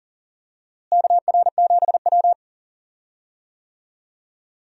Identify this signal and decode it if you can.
Morse code, heard as a keyed tone, about 40 wpm, 700 Hz. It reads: KR7W
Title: USA Call Signs - 4 Characters - Repeated 3x - 40wpm